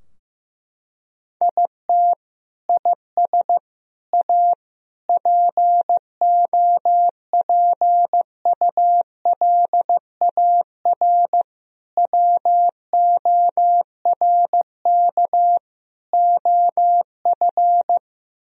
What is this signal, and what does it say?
Morse code, heard as a keyed tone, about 15 wpm, 700 Hz. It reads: IT IS A POPULAR WORK OF
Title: Sentences from Top 1000 Words - Review Part 3 - 15wpm